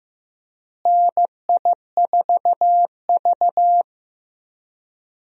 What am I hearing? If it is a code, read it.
Morse code, heard as a keyed tone, about 15 wpm, 700 Hz. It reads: NI4V
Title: USA Call Signs - 4 Characters - Repeated 2x - 15wpm